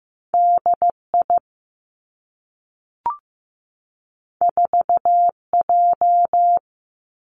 Morse code, heard as a keyed tone, about 15 wpm, 700 Hz. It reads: DI 4J
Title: Two Challenging Letter-Number Combinations 15wpm